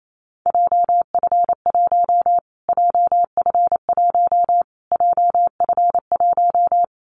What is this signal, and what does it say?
Morse code, heard as a keyed tone, about 28 wpm, 700 Hz. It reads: JF1 JF1 JF1
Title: Callsign Prefixes for Top 25 DXCC - Repeated 3x - 28wpm